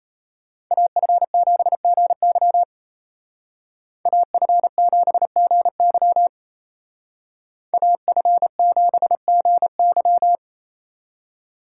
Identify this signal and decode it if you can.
Morse code, heard as a keyed tone, about 34 wpm, 700 Hz. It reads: AF7GY AF7GY AF7GY
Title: Speed Racing - USA 2x2 Call Signs 28wpm